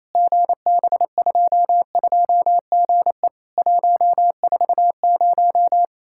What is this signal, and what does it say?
Morse code, heard as a keyed tone, about 28 wpm, 700 Hz. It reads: GB22GE 140
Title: Worked All Europe DX Contest - QTC Exchange - 28wpm